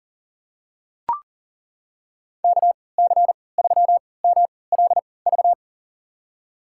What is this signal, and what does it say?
Morse code, heard as a keyed tone, about 40 wpm, spaced as for 23 wpm, 700 Hz. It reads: KC3MLV
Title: USA Call Signs - 6 Characters - 3x Character Spacing - 40wpm